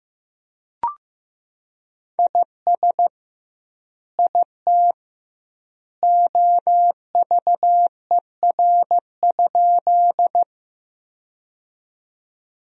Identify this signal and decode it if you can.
Morse code, heard as a keyed tone, about 15 wpm, 700 Hz. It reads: IS IT OVER?
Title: 2x Word-Spacing - Sentences from Top 100 Words - 15wpm